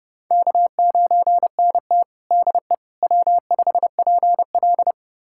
Morse code, heard as a keyed tone, about 30 wpm, 700 Hz. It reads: K9NT DE W5PL
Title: USA Callsign DE Callsign - 4 Characters - 30wpm